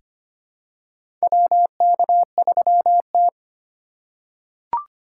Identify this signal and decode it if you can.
Morse code, heard as a keyed tone, about 25 wpm, 700 Hz. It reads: WK3T
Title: USA Call Signs - 4 Characters - 3x Character Spacing - 25wpm